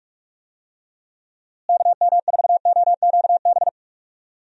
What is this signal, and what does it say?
Morse code, heard as a keyed tone, about 46 wpm, 700 Hz. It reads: KM4OQB
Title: USA Call Signs - 6 Characters - Repeated 2x - 45wpm